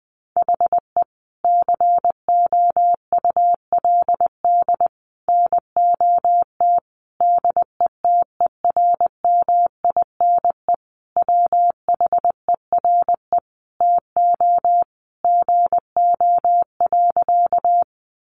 Morse code, heard as a keyed tone, about 20 wpm, 700 Hz. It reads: HE COULD NOT DETERMINE WHERE TO GO.
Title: Sentences from Top 700 Words 20wpm